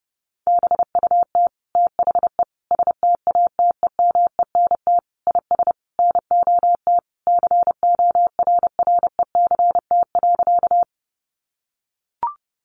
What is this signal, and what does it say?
Morse code, heard as a keyed tone, about 30 wpm, 700 Hz. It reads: BUT THE STATEMENT IS NOT CORRECT.
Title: Sentences from Top 1200 Words 30wpm